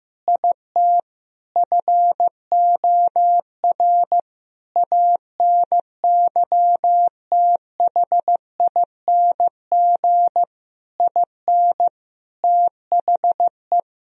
Morse code, heard as a keyed tone, about 15 wpm, 700 Hz. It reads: IT FOR ANYTHING IN THE
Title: Sentences from Top 1000 Words - Review Part 3 - 15wpm